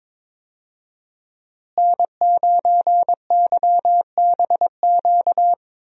Morse code, heard as a keyed tone, about 22 wpm, 700 Hz. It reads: N9YBQ
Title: USA Call Signs - 5 Characters - Repeated 3x - 22wpm